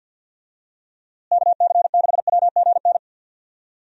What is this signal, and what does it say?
Morse code, heard as a keyed tone, about 51 wpm, 700 Hz. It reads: KC6WGN
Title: USA Call Signs - 6 Characters - Repeated 2x - 50wpm